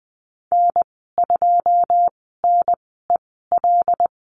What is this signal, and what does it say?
Morse code, heard as a keyed tone, about 20 wpm, spaced as for 15 wpm, 700 Hz. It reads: N2NEL
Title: 154 Lesson 25g - Letter p - Call Signs 20wpm